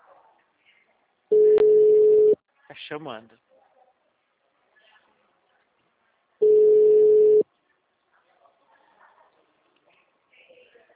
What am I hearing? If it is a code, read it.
Ringback tone (Europe, 425 Hz), with speech.
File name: out- -2006- - - .79383